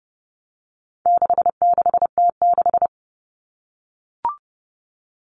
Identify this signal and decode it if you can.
Morse code, heard as a keyed tone, about 28 wpm, 700 Hz. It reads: 66T6
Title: Common Cut Numbers - 4-Digit 30wpm